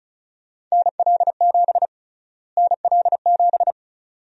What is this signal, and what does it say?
Morse code, heard as a keyed tone, about 36 wpm, 700 Hz. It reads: NL7 NL7
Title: USA Call Sign Prefixes - Repeated 2x - 35wpm